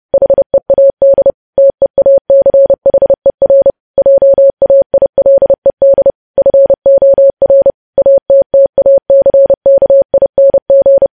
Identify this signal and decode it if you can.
Morse code, heard as a keyed tone, about 30 wpm, 550 Hz. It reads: HEAD TEACHER JAILED FOR ATTACKING
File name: podcast / News headlines-1x-30x30